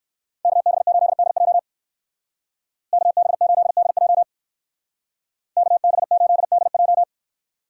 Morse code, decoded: KB8DJ KB8DJ KB8DJ

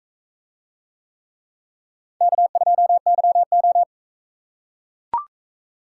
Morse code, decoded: K2YO

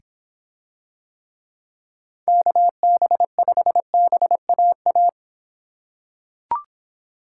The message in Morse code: KB5BAA

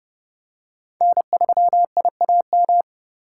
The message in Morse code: N3IAM

